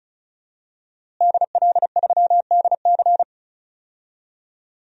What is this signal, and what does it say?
Morse code, heard as a keyed tone, about 36 wpm, 700 Hz. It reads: DL3DC
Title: DXCC-100 Call Signs - 5 Characters - Repeated 3x - 35wpm